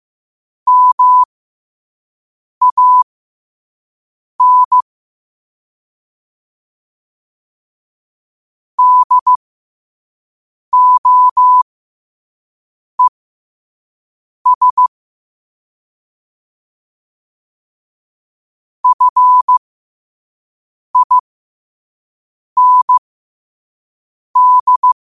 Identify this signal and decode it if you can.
Morse code, heard as a keyed tone, about 15 wpm, spaced as for 5 wpm, 1000 Hz. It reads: MAN DOES FIND